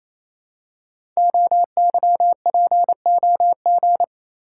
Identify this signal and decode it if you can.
Morse code, heard as a keyed tone, about 28 wpm, 690 Hz. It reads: OYPOG
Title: Code Groups - 5 Letters - 28wpm